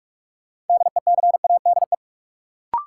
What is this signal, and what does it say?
Morse code, heard as a keyed tone, about 46 wpm, 700 Hz. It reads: DECADE
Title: Sets of 5 Words - New Word Drill - 45wpm